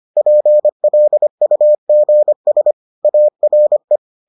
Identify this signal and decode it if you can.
Morse code, heard as a keyed tone, about 25 wpm, 600 Hz. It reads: PLUGS ARE